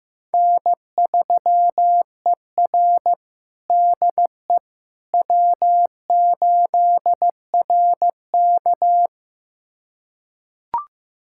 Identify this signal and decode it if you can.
Morse code, heard as a keyed tone, about 15 wpm, 700 Hz. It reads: N3ER DE W8RK
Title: USA Callsign DE Callsign - 4 Characters - 15wpm